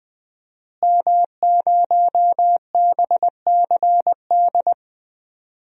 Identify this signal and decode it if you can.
Morse code, heard as a keyed tone, about 21 wpm, 700 Hz. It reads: M0BCD
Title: DXCC-100 Call Signs - 5 Characters - Repeated 3x - 20wpm